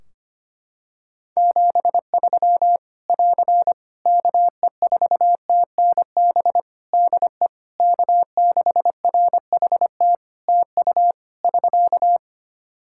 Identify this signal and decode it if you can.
Morse code, heard as a keyed tone, about 25 wpm, 700 Hz. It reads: 73 <AR> KE4TNB DE K6RHT TU <SK>
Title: Practice QSO - Exchange 4 - Ending - 25wpm